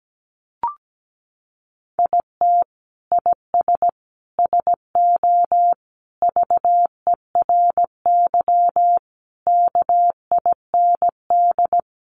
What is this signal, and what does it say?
Morse code, heard as a keyed tone, about 17 wpm, 700 Hz. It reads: IT IS SO VERY KIND
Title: Sentences from Top 300 Words 17wpm